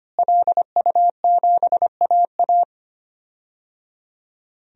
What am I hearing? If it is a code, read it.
Morse code, heard as a keyed tone, about 25 wpm, 700 Hz. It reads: LU7AA
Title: DXCC-100 Call Signs - 5 Characters - Repeated 3x - 25wpm